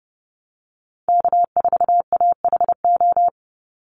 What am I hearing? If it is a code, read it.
Morse code, heard as a keyed tone, about 30 wpm, 700 Hz. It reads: K4AHO
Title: Call Signs Repeated 3x - 30wpm